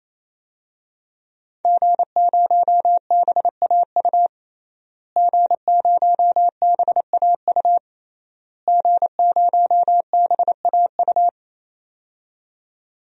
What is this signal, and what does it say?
Morse code, heard as a keyed tone, about 28 wpm, 700 Hz. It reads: G0BAU G0BAU G0BAU
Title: DXCC-100 Call Signs - 5 Characters - Repeated 3x - 28wpm